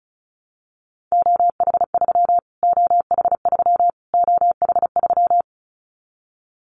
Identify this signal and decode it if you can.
Morse code, heard as a keyed tone, about 34 wpm, 700 Hz. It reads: OH3 OH3 OH3
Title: Callsign Prefixes for Top 25 DXCC - Repeated 3x - 35wpm